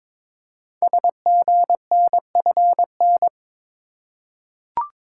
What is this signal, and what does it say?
Morse code, heard as a keyed tone, about 21 wpm, 700 Hz. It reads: SGNFN
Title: Code Groups - 5 Letters - 22wpm